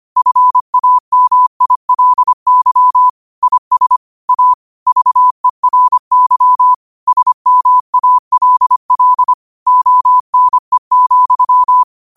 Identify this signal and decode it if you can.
Morse code, heard as a keyed tone, about 25 wpm, 1000 Hz. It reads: FAMILY IS A VERY SMALL ONE,
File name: qotdfiles / 25wpm